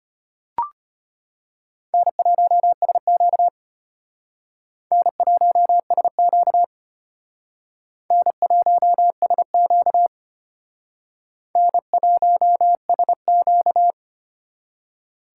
Morse code, heard as a keyed tone, about 33 wpm, 700 Hz. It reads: N1SQ N1SQ N1SQ N1SQ